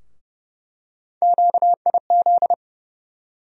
Morse code, heard as a keyed tone, about 31 wpm, 700 Hz. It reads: QIZ
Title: USA Call Sign Postfixes - 3 Letters 30wpm